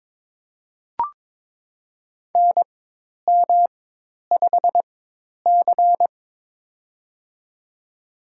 Morse code, heard as a keyed tone, about 21 wpm, spaced as for 10 wpm, 700 Hz. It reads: NM5C
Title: USA Call Signs - 4x Character Spacing - 22wpm